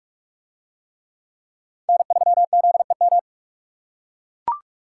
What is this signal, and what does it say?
Morse code, heard as a keyed tone, about 42 wpm, 700 Hz. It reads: N3ZEM